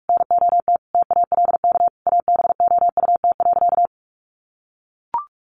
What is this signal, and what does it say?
Morse code, heard as a keyed tone, about 44 wpm, 700 Hz. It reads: NOT TALK ABOUT.
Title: Sentences from Top 800 Words 45wpm